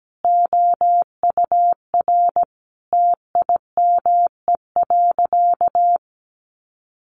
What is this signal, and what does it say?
Morse code, heard as a keyed tone, about 17 wpm, 700 Hz. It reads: OUR TIME.
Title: Sentences from Top 1400 Words 17wpm